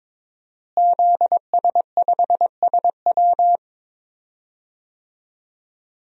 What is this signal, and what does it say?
Morse code, heard as a keyed tone, about 22 wpm, 700 Hz. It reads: ZS5SW
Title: DXCC-100 Call Signs - 5 Characters - Repeated 2x - 22wpm